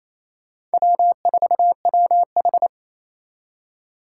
Morse code, heard as a keyed tone, about 36 wpm, 700 Hz. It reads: W4WH